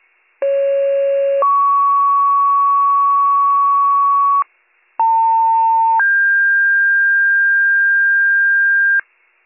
Two-tone page, heard as a one-second tone, then a three-second tone, about 550 then 1120 Hz, about 900 then 1600 Hz, with noise. It is heard from 0.4 to 9.0 s.